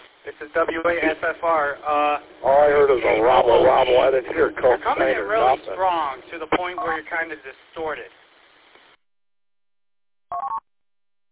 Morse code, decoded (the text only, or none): none